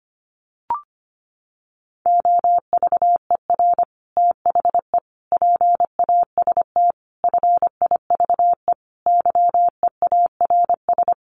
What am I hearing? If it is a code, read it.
Morse code, heard as a keyed tone, about 25 wpm, 700 Hz. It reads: OVER THE PAST FIVE YEARS
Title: Sets of 5 Words Encore 25wpm